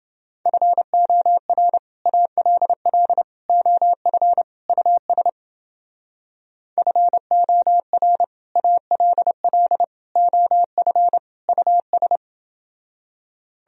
Morse code, heard as a keyed tone, about 28 wpm, 700 Hz. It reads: FOR ALL OF US FOR ALL OF US